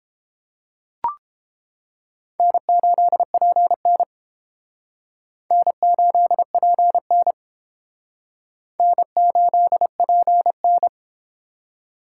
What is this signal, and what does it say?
Morse code, heard as a keyed tone, about 29 wpm, 700 Hz. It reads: N8PN N8PN N8PN